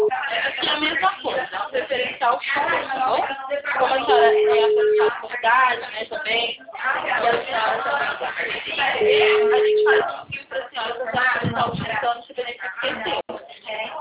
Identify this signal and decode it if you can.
Ringback tone (Europe, 425 Hz), with speech.